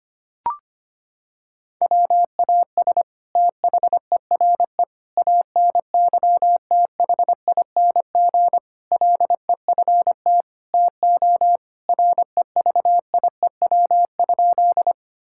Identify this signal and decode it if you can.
Morse code, heard as a keyed tone, about 25 wpm, 700 Hz. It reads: WAS THERE ANYTHING LEFT TO REVIEW?
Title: Sentences from Top 1000 Words - Review Part 6 - 25wpm